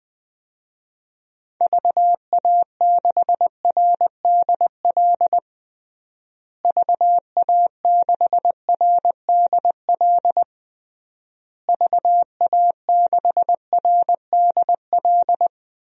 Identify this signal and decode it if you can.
Morse code, heard as a keyed tone, about 19 wpm, 700 Hz. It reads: VA6RDL VA6RDL VA6RDL
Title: DXCC-100 Call Signs - 6 Characters - Repeated 3x - 20wpm